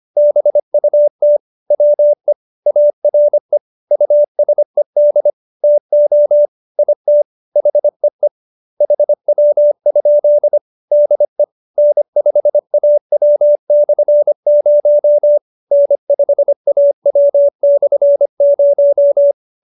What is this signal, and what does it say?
Morse code, heard as a keyed tone, about 25 wpm, 590 Hz. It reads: BUT WE ARE USED TO IT HEE HW? DE N5AW/0 N5AW/0
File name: QSO312 25